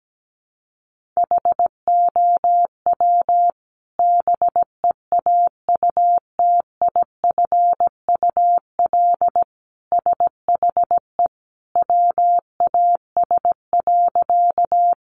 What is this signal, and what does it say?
Morse code, heard as a keyed tone, about 17 wpm, 700 Hz. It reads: HOW BEAUTIFUL SHE WAS.
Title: Sentences from Top 900 Words 17wpm